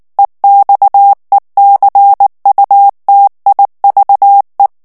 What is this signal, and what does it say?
Morse code, heard as a keyed tone, about 19 wpm, 800 Hz. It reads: EXECUTIVE
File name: morse mp3 / ANGLAIS texte clair 3 vitesse 18 mots minute